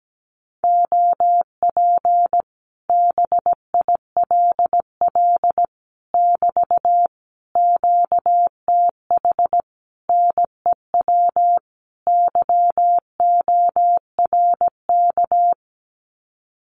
Morse code, heard as a keyed tone, about 17 wpm, 700 Hz. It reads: OP BILL <BT> QTH NEW YORK
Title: Common QSO Phrases Encore 17wpm